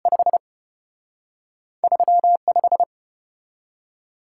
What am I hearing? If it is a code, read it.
Morse code, heard as a keyed tone, about 34 wpm, 700 Hz. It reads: 5 35